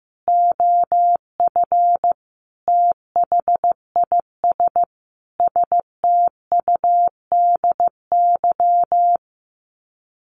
Morse code, heard as a keyed tone, about 15 wpm, 700 Hz. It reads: OF THIS STUDY
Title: Sets of 5 Words 15wpm